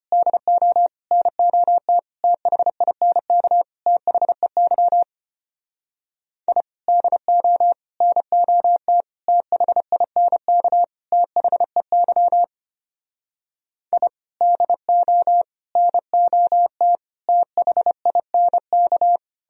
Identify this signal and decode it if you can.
Morse code, heard as a keyed tone, about 31 wpm, 700 Hz. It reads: DO NOT THINK THEY I DO NOT THINK THEY I DO NOT THINK
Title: Speed-Racing - Sets of 5 Words 25wpm